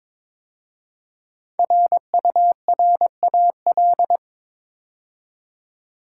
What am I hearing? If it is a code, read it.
Morse code, heard as a keyed tone, about 22 wpm, 690 Hz. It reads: RURAL
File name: 5 Letter Words - First 3 Letters Given - 22wpm